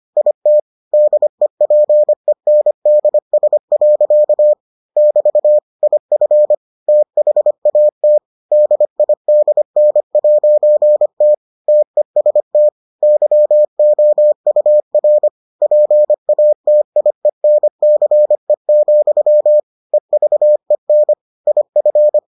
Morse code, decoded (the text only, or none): IT DEPENDS. <BT> IF THAT DIDN'T TEST YOUR PATIENCE, EVEN IF